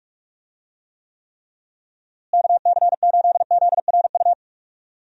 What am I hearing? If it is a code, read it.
Morse code, heard as a keyed tone, about 46 wpm, 700 Hz. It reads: KC8ZRU
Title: USA Call Signs - 6 Characters - Repeated 3x - 45wpm